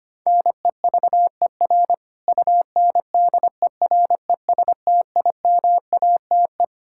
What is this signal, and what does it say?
Morse code, heard as a keyed tone, about 25 wpm, 700 Hz. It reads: NEVER UNDERESTIMATE
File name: practice / 25wpm / Sentences with 4-Word Phrases - 25wpm